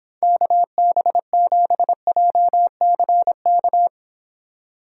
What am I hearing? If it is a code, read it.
Morse code, heard as a keyed tone, about 26 wpm, 700 Hz. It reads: KB7JCK